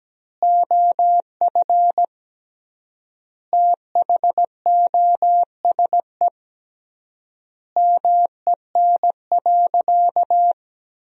Morse code, heard as a keyed tone, about 17 wpm, 700 Hz. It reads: OF THOSE MEN.
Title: 3x Word-Spacing - Sentences from Top 300 Words - 17wpm